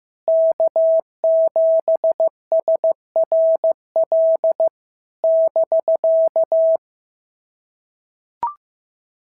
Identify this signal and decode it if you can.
Morse code, heard as a keyed tone, about 14 wpm, 650 Hz. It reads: K7SRL <BK>
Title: SOTA QSO Practice 15wpm